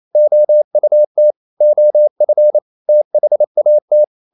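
Morse code, heard as a keyed tone, about 28 wpm, 600 Hz. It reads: OUT OF THAT
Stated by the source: Morse code practice at 28 wpm, text of Ken Follett's novel Never in English, from CW med Gnister.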